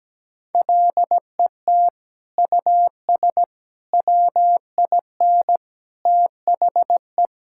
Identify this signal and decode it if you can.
Morse code, heard as a keyed tone, about 17 wpm, 700 Hz. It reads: LET US WIN THE
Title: Sentences from Top 1600 Words 17wpm